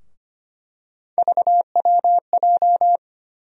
Morse code, heard as a keyed tone, about 25 wpm, 700 Hz. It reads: VWJ
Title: USA Call Sign Postfixes - 3 Letters 25wpm